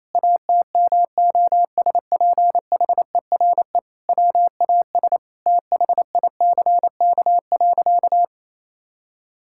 Morse code, heard as a keyed tone, about 28 wpm, 700 Hz. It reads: ATMOSPHERE WAS THICK.
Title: Sentences from Top 1600 Words 28wpm